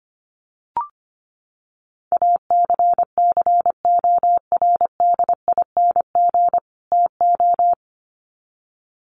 Morse code, heard as a keyed tone, about 25 wpm, 700 Hz. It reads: ACCORDING TO
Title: Sets of 2 Words 25wpm